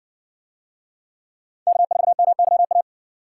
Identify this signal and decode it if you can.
Morse code, heard as a keyed tone, about 58 wpm, 700 Hz. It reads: K4MYA